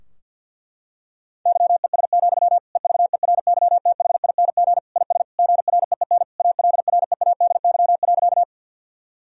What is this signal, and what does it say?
Morse code, heard as a keyed tone, about 52 wpm, 700 Hz. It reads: YES, EVERYTHING IS GREEN ALREADY.
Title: Sentences from Top 1000 Words - Review Part 2 - 50wpm